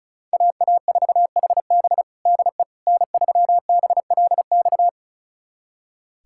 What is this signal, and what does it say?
Morse code, heard as a keyed tone, about 36 wpm, 700 Hz. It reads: AA4HB DE N3BLX
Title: USA Callsign DE Callsign - 5 Characters - 35wpm